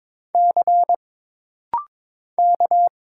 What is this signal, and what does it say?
Morse code, heard as a keyed tone, about 22 wpm, 700 Hz. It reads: C K
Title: Single Letter-Number - Rapid-Fire 22wpm